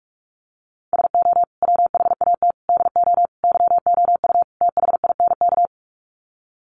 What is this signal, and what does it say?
Morse code, heard as a keyed tone, about 44 wpm, 700 Hz. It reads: SO WHAT DO YOU THINK
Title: Sets of 5 Words Encore2 45wpm